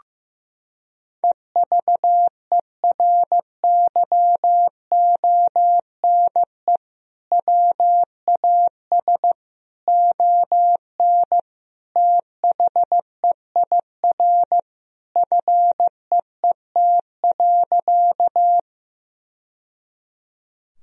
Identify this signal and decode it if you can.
Morse code, heard as a keyed tone, about 15 wpm, 700 Hz. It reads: EVERYONE WAS ON THEIR FEET.
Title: Sentences from Top 1000 Words - Review Part 1 - 15wpm